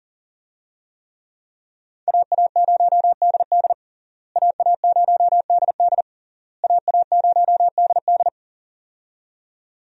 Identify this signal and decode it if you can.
Morse code, heard as a keyed tone, about 40 wpm, 700 Hz. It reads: AA0DD AA0DD AA0DD
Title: USA Call Signs - 5 Characters - Repeated 3x - 40wpm